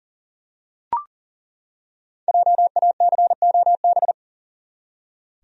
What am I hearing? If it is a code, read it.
Morse code, heard as a keyed tone, about 40 wpm, 700 Hz. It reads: JACOB
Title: USA Common Names - 40wpm